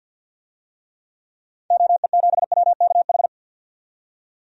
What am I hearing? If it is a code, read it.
Morse code, heard as a keyed tone, about 52 wpm, 700 Hz. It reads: OE7WKH
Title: DXCC-100 Call Signs - 6 Characters - 50wpm